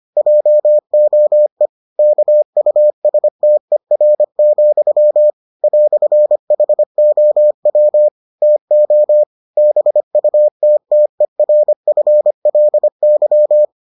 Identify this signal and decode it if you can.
Morse code, heard as a keyed tone, about 25 wpm, 600 Hz. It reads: JOE KUSTER, "HOW TO BUTTERFLY